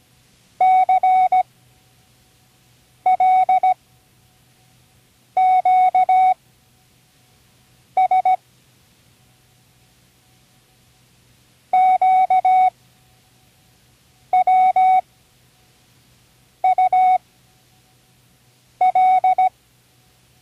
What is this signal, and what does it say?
Morse code, heard as a keyed tone, about 17 wpm, spaced as for 5 wpm, 750 Hz. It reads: CLQS QWUL
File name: CWResources / Learning Morse Code (from Tune in the World) / 24 random